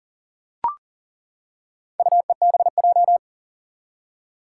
Morse code, heard as a keyed tone, about 39 wpm, 700 Hz. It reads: UEBJ
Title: Four Challenging Letter-Number Combinations 40wpm